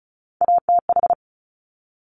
Morse code, heard as a keyed tone, about 36 wpm, 700 Hz. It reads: ATH
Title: Top Three Letter Combinations in Words 35wpm